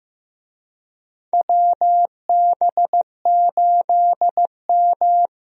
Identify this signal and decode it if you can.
Morse code, heard as a keyed tone, about 15 wpm, 700 Hz. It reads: WB8M
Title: USA Call Signs - 4 Characters - Repeated 3x - 15wpm